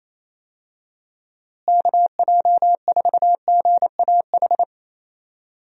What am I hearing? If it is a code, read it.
Morse code, heard as a keyed tone, about 28 wpm, 700 Hz. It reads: KJ4GAH
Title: USA Call Signs - 6 Characters - Repeated 3x - 28wpm